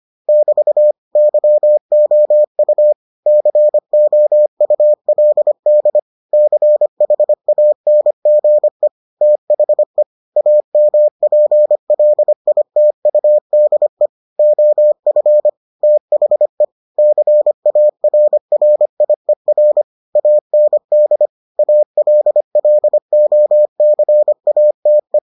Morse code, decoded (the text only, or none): <BT> YOU COULD CHANGE THE AMPLITUDE OF THE CARRIER AND ALLOCATE